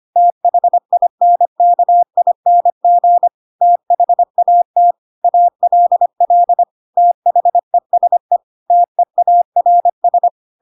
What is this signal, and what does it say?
Morse code, heard as a keyed tone, about 25 wpm, 700 Hz. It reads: THINKING THAT ALL THESE TEARS
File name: Pinocchio / Pinocchio ch7 25 25 ik1hge